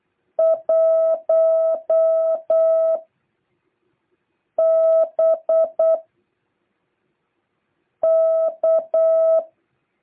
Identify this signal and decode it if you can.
Morse code, heard as a keyed tone, about 8 wpm, 650 Hz. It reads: 1 B K